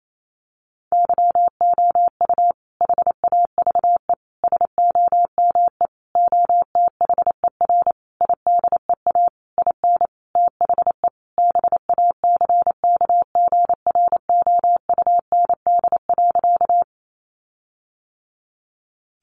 Morse code, heard as a keyed tone, about 28 wpm, 700 Hz. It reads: YOU HAVE SOME OTHER IDEA IN THE BACKGROUND.
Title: Sentences from Top 1300 Words 28wpm